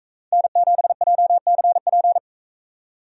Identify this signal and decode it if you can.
Morse code, heard as a keyed tone, about 43 wpm, 700 Hz. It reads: N7JCP